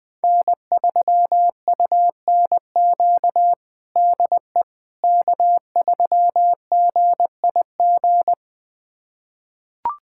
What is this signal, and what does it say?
Morse code, heard as a keyed tone, about 20 wpm, 700 Hz. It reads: N3UNQ DE K3GIG
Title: USA Callsign DE Callsign - 5 Characters - 20wpm